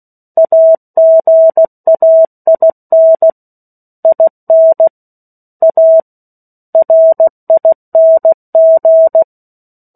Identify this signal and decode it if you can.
Morse code, heard as a keyed tone, about 16 wpm, 650 Hz. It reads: AGAIN IN A RING
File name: 03-Chapter-03 16 WPM